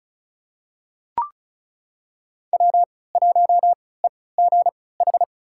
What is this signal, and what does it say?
Morse code, heard as a keyed tone, about 36 wpm, spaced as for 20 wpm, 700 Hz. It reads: W1EGH
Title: USA Call Signs - 5 Characters - 3x Character Spacing - 35wpm